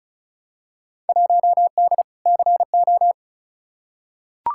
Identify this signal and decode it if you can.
Morse code, heard as a keyed tone, about 34 wpm, 700 Hz. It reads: 1D CO
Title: Field Day Exchanges - 35wpm